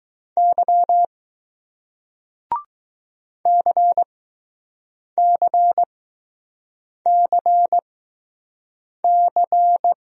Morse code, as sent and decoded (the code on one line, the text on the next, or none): -.-- / -.-. -.-. -.-. -.-.
Y CCCC